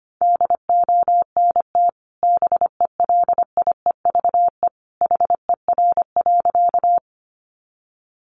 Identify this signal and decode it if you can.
Morse code, heard as a keyed tone, about 25 wpm, 700 Hz. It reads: DONT BELIEVE HER.
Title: Sentences from Top 1000 Words 25wpm